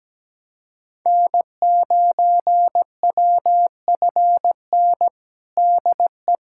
Morse code, heard as a keyed tone, about 17 wpm, 700 Hz. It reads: N9WFN DE
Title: USA Callsign DE Callsign - 5 Characters - 17wpm